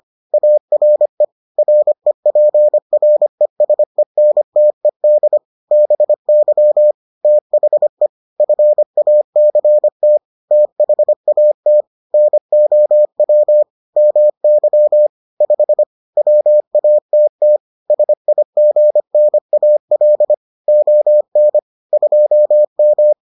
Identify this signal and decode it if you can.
Morse code, heard as a keyed tone, about 25 wpm, 600 Hz. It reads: ARE REPRESENTED BY THE FACT THAT NOW MY 5 WATT SIGNAL ON 2M